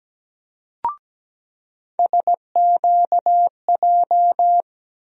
Morse code, heard as a keyed tone, about 17 wpm, 700 Hz. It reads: SQJ
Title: USA Call Sign Postfixes - 3 Letters 17wpm